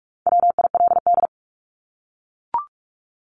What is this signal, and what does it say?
Morse code, heard as a keyed tone, about 44 wpm, 700 Hz. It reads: WILD
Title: Top 1501-1750 Words 45wpm